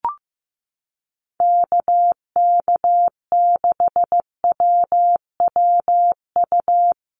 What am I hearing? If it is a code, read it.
Morse code, heard as a keyed tone, about 15 wpm, 700 Hz. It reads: KK6WWU
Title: USA Call Signs Repeated 3x 15wpm